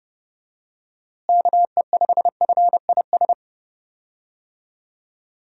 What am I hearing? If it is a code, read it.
Morse code, heard as a keyed tone, about 31 wpm, 700 Hz. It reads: KE5FIS
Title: USA Call Signs - 6 Characters - Repeated 2x - 30wpm